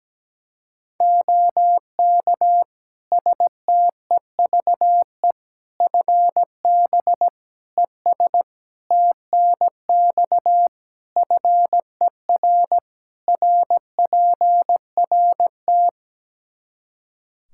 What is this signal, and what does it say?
Morse code, heard as a keyed tone, about 17 wpm, 700 Hz. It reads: OK STEVE FB ES TNX FER RPRT
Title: Practice QSO - Exchange 2 - Info - 17wpm